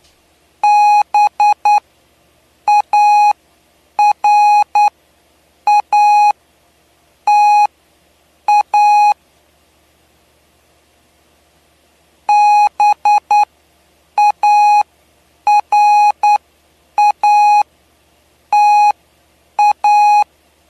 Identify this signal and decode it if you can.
Morse code, heard as a keyed tone, about 9 wpm, spaced as for 7 wpm, 830 Hz. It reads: BARATA BARATA
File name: cw / py2lcd / 07-PY2LCD-Li��o 7